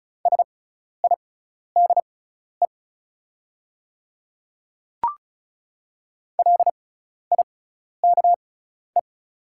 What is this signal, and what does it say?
Morse code, heard as a keyed tone, about 35 wpm, spaced as for 12 wpm, 700 Hz. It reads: SIDE LIKE
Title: 6x Character Spacing - Top 100 Words - 35wpm